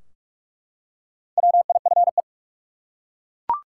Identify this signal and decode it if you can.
Morse code, heard as a keyed tone, about 42 wpm, 700 Hz. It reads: WIFE
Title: Four Letter Words - 45wpm